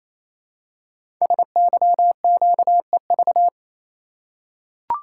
Morse code, heard as a keyed tone, about 27 wpm, 700 Hz. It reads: SYQEV
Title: Code Groups - 5 Letters - 28wpm